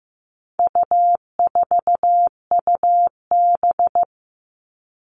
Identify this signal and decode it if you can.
Morse code, heard as a keyed tone, about 15 wpm, 700 Hz. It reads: U4UB
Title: All Cut Numbers - 4-Digit 15wpm